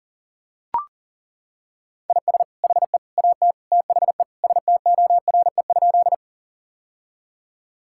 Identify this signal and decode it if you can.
Morse code, heard as a keyed tone, about 37 wpm, 700 Hz. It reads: IS HE AT THE STORE?
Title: Sentences from Top 1000 Words - Review Part 5 - 40wpm